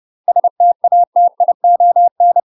Morse code, heard as a keyed tone, about 30 wpm, 700 Hz. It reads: STATION
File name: UK MASTER QSOs TRAINING/30WPM